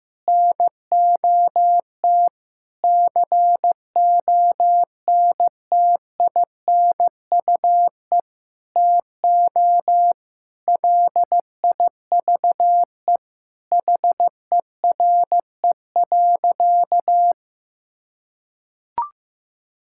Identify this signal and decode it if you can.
Morse code, heard as a keyed tone, about 15 wpm, 700 Hz. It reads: NOT CONTINUE TO LIVE HERE.
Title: Sentences from Top 1000 Words - Review Part 6 - 15wpm